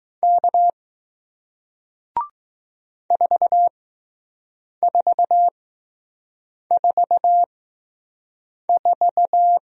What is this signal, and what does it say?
Morse code, heard as a keyed tone, about 20 wpm, spaced as for 6 wpm, 700 Hz. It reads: K 4444